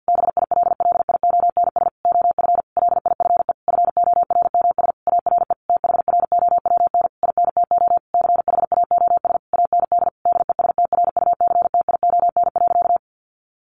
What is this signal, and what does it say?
Morse code, heard as a keyed tone, about 50 wpm, 700 Hz. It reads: BILLIONS OF LIFE FORMS ARE THROWN INTO CHAOS AND DESTRUCTION.
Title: A Sound Of Thunder 50wpm